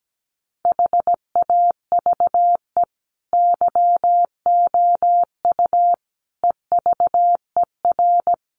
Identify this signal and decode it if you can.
Morse code, heard as a keyed tone, about 17 wpm, 700 Hz. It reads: HAVE YOU EVER
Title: Sentences from Top 1500 Words 17wpm